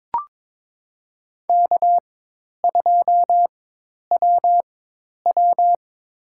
Morse code, heard as a keyed tone, about 22 wpm, spaced as for 10 wpm, 700 Hz. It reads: K2WW
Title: USA Call Signs - 4 Characters - 4x Character Spacing - 22wpm